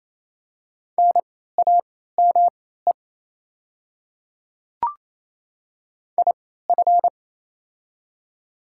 Morse code, heard as a keyed tone, about 28 wpm, spaced as for 16 wpm, 700 Hz. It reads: NAME IF
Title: 3x Character Spacing - Top 200 Words - 28wpm